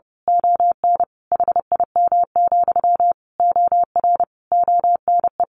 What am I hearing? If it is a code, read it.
Morse code, heard as a keyed tone, about 30 wpm, 700 Hz. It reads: ON HIM, OR ONE
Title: A Sound Of Thunder 30wpm